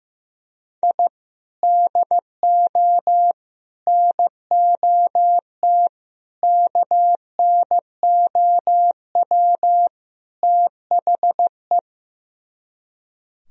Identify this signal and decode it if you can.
Morse code, heard as a keyed tone, about 15 wpm, 700 Hz. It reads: I DO NOT KNOW THE
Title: Speed-Racing - Sets of 5 Words 15wpm